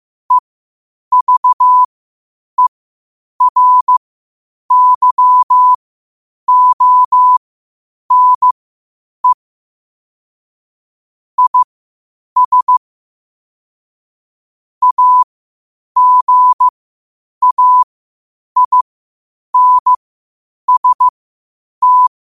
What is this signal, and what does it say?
Morse code, heard as a keyed tone, about 15 wpm, spaced as for 8 wpm, 1000 Hz. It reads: EVERYONE IS AGAINST
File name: qotdfiles / 8wpm